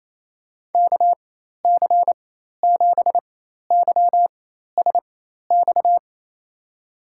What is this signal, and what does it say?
Morse code, heard as a keyed tone, about 28 wpm, spaced as for 13 wpm, 700 Hz. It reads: KC7YSX
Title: USA Call Signs - 6 Characters - 4x Character Spacing - 28wpm